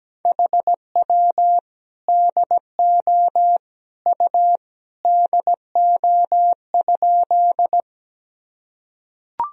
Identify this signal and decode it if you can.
Morse code, heard as a keyed tone, about 17 wpm, 700 Hz. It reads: HW DO U DO?
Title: Sentences from Top 100 Words and Abbreviations 17wpm